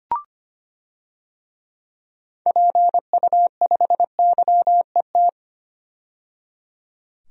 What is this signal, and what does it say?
Morse code, heard as a keyed tone, about 25 wpm, 700 Hz. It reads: PU5YET
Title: DXCC-100 Call Signs - 6 Characters - 25wpm